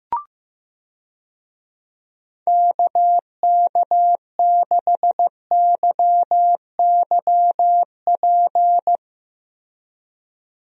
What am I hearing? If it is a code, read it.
Morse code, heard as a keyed tone, about 15 wpm, 700 Hz. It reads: KK6YYP
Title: USA Call Signs - 6 Characters - Repeated 2x - 15wpm